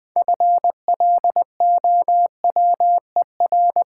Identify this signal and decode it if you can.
Morse code, heard as a keyed tone, about 20 wpm, 700 Hz. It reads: FLOWER
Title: Build A Word - 3 Letters At A Time - 6-Letter Words - 20wpm